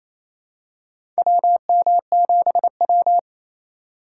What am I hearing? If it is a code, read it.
Morse code, heard as a keyed tone, about 28 wpm, 700 Hz. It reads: WM7W
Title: USA Call Signs - 4 Characters - 2x Character Spacing - 28wpm